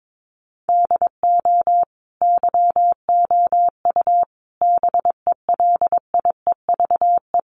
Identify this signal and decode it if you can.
Morse code, heard as a keyed tone, about 22 wpm, 700 Hz. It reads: DO YOU BELIEVE
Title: Sentences from Top 1500 Words 22wpm